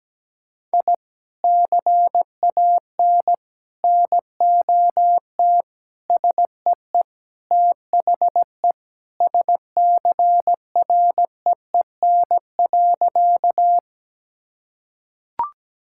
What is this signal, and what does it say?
Morse code, heard as a keyed tone, about 17 wpm, 700 Hz. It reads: I CAN NOT SEE THE SCREEN.
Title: Sentences from Top 1000 Words - Review Part 6 - 17wpm